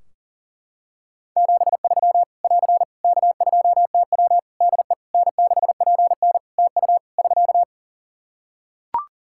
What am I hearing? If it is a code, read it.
Morse code, heard as a keyed tone, about 40 wpm, 700 Hz. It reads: 73 <AR> K2TW DE N6PN TU <SK>
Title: Practice QSO - Exchange 4 - Ending - 40wpm